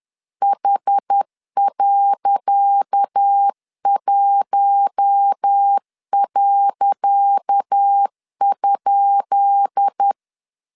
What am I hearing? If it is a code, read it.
Morse code, heard as a keyed tone, about 10 wpm, 800 Hz. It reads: H.1.?
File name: LEZIONI / 10 SIGNS / 10C DOT 10wpm 2940